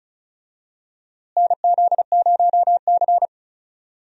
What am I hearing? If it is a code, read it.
Morse code, heard as a keyed tone, about 36 wpm, 700 Hz. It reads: NZ0C
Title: USA Call Signs - 4 Characters - Repeated 2x - 35wpm